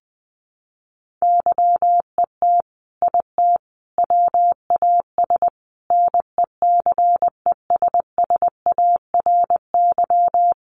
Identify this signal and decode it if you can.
Morse code, heard as a keyed tone, about 20 wpm, 700 Hz. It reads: YET IT WAS NECESSARY
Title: Sentences from Top 800 Words 20wpm